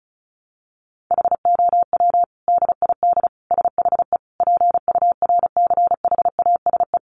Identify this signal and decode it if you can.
Morse code, heard as a keyed tone, about 34 wpm, 700 Hz. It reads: HOW DID SHE PURCHASE
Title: Sentences from Top 800 Words 35wpm